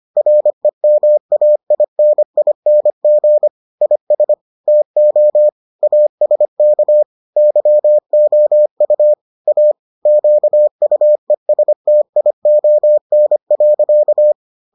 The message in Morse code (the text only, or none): REMAINING IS TO ASK YOU A QUESTION.